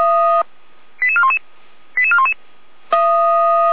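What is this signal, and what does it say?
Five-tone selective call (ZVEI), heard as quick runs of five tones, tones of about 80 ms. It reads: 88411 88411
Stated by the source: ZVEI five-tone call 88311 received by SDR, recorded by Feuerwehr Orlamünde.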